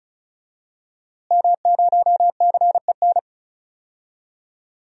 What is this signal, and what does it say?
Morse code, heard as a keyed tone, about 36 wpm, 700 Hz. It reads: M0CEN
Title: DXCC-100 Call Signs - 5 Characters - Repeated 2x - 35wpm